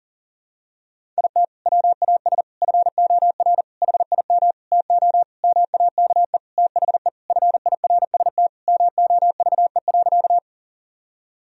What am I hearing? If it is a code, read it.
Morse code, heard as a keyed tone, about 40 wpm, 700 Hz. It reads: IT WAS FOR HIM TO MAKE THE FIRST MOVE.
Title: Sentences from Top 1000 Words - Review Part 1 - 40wpm